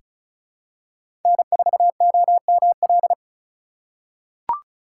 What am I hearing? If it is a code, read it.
Morse code, heard as a keyed tone, about 36 wpm, 690 Hz. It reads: N4OML
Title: USA Call Signs - 5 Characters - 3x Character Spacing - 35wpm